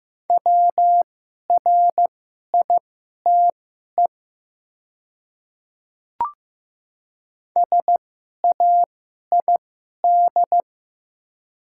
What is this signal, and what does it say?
Morse code, heard as a keyed tone, about 15 wpm, spaced as for 11 wpm, 700 Hz. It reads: WRITE SAID
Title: 2x Character Spacing - Top 100 Words - 15wpm